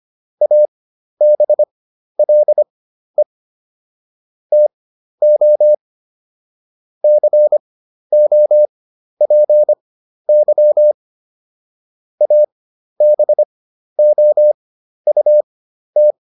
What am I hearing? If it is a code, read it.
Morse code, heard as a keyed tone, about 25 wpm, spaced as for 12 wpm, 600 Hz. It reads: ABLE TO COPY ABOUT